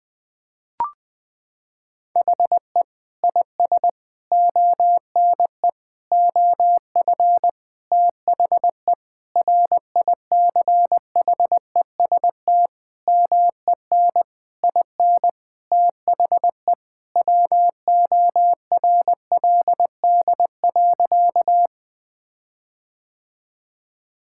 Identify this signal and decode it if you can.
Morse code, heard as a keyed tone, about 20 wpm, 700 Hz. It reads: HE IS ONE OF THE RICHEST MEN IN THE WORLD.
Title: Sentences with 8-Word Phrases - 20wpm